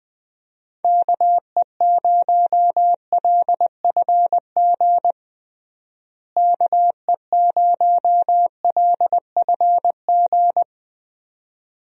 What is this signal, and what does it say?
Morse code, heard as a keyed tone, about 20 wpm, 700 Hz. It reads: KE0LFG KE0LFG
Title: USA Call Signs - 6 Characters - Repeated 2x - 20wpm